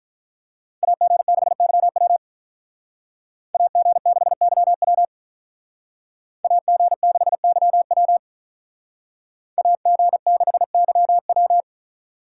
Morse code, heard as a keyed tone, about 45 wpm, 700 Hz. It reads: AG6YW AG6YW AG6YW AG6YW